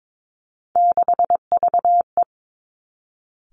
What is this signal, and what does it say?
Morse code, heard as a keyed tone, about 22 wpm, 700 Hz. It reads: 6VE
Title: Three Challenging Letter-Number Combinations 22wpm